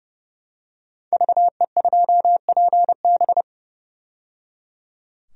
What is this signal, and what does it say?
Morse code, heard as a keyed tone, about 30 wpm, 700 Hz. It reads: VE2PB
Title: DXCC-100 Call Signs - 5 Characters - 30wpm